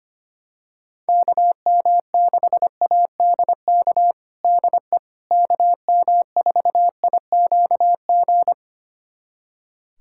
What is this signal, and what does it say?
Morse code, heard as a keyed tone, about 25 wpm, 700 Hz. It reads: KM6ADK DE KM4IQG
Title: USA Callsign DE Callsign - 6 Characters - 25wpm